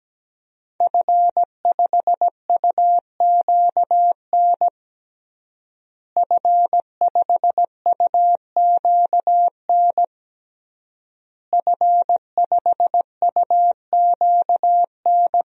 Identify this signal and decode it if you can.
Morse code, heard as a keyed tone, about 17 wpm, 700 Hz. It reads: F5UQN F5UQN F5UQN
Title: DXCC-100 Call Signs - 5 Characters - Repeated 3x - 17wpm